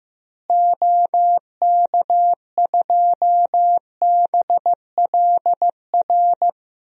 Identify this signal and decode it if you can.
Morse code, heard as a keyed tone, about 15 wpm, 700 Hz. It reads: OK2BLR
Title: DXCC-100 Call Signs - 6 Characters - Repeated 3x - 15wpm